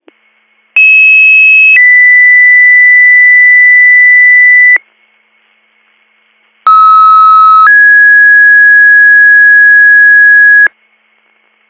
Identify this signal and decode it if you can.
Two-tone page, heard as a one-second tone, then a three-second tone, about 2600 then 2000 Hz, about 1290 then 1650 Hz. It is heard from 0.8 to 10.7 s.